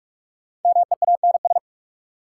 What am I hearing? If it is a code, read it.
Morse code, heard as a keyed tone, about 46 wpm, 700 Hz. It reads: MEANS
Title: Sets of 5 Words - New Word Drill - 45wpm